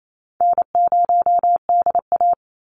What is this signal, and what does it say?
Morse code, heard as a keyed tone, about 28 wpm, 700 Hz. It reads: N0DA
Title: NA SOTA Chaser Callsigns 28wpm